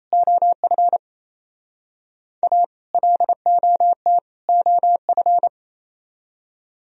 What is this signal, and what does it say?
Morse code, heard as a keyed tone, about 33 wpm, 700 Hz. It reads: OF A LOT OF
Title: Speed-Racing - Sets of 3 Words 28wpm